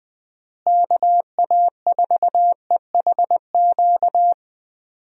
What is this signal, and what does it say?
Morse code, heard as a keyed tone, about 20 wpm, 700 Hz. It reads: KA4EHQ
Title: USA Call Signs - 6 Characters - Repeated 3x - 20wpm